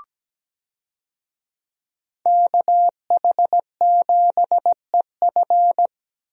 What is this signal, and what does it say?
Morse code, heard as a keyed tone, about 17 wpm, 700 Hz. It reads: KH7EF
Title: USA Call Signs - 5 Characters - Repeated 2x - 17wpm